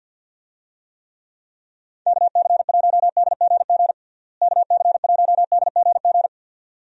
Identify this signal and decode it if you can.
Morse code, heard as a keyed tone, about 51 wpm, 700 Hz. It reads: KC1DGG KC1DGG
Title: USA Call Signs - 6 Characters - Repeated 2x - 50wpm